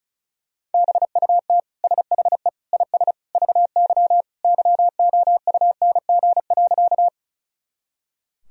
Morse code, heard as a keyed tone, about 36 wpm, 700 Hz. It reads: BUT SHE IS VY YOUNG.
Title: Sentences from Top 400 Words and Abbreviations 35wpm